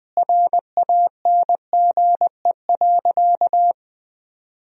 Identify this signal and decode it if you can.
Morse code, heard as a keyed tone, about 21 wpm, 700 Hz. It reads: RANGE.
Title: Sentences from Top 1000 Words - Review Part 4 - 20wpm